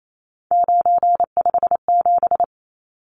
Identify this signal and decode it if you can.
Morse code, heard as a keyed tone, about 28 wpm, 700 Hz. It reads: 957
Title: Numbers - 3-Digit 28wpm